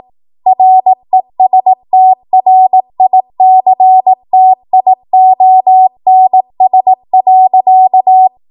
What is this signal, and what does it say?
Morse code, heard as a keyed tone, about 18 wpm, 750 Hz. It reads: RESTRICTIONS.
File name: Morse / 18WPM